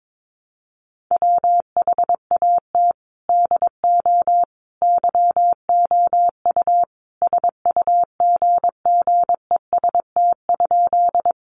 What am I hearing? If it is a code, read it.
Morse code, heard as a keyed tone, about 22 wpm, 700 Hz. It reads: WHAT DO YOU SUGGEST?
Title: Sentences from Top 1300 Words 22wpm